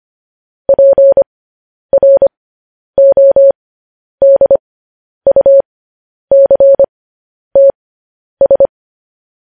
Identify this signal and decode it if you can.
Morse code, heard as a keyed tone, about 25 wpm, spaced as for 10 wpm, 550 Hz. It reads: PRODUCTS
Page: news headlines in morse code at 10 wpm